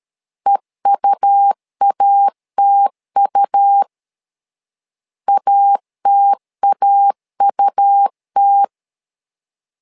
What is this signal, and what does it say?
Morse code, heard as a keyed tone, about 13 wpm, 800 Hz. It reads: EUATU ATAUT
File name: LEZIONI / 2 UHDN / 2 U 12wpm 1338